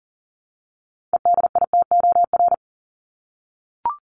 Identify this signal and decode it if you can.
Morse code, heard as a keyed tone, about 40 wpm, 700 Hz. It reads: EDITOR
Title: Top 1501-1750 Words 40wpm